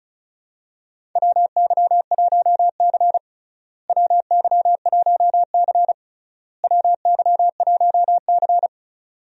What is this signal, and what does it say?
Morse code, heard as a keyed tone, about 34 wpm, 700 Hz. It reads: WY1C WY1C WY1C
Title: USA Call Signs - 4 Characters - Repeated 3x - 35wpm